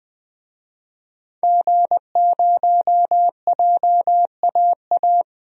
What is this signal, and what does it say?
Morse code, heard as a keyed tone, about 20 wpm, 700 Hz. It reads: G0JAA
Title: DXCC-100 Call Signs - 5 Characters - 20wpm